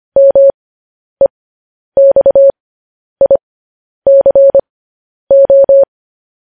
Morse code, heard as a keyed tone, about 25 wpm, spaced as for 10 wpm, 550 Hz. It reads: MEXICO